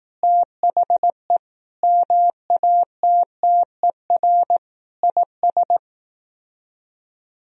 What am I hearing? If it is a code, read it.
Morse code, heard as a keyed tone, about 18 wpm, 700 Hz. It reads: THE MATTER IS
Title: Speed-Racing - Sets of 5 Words 15wpm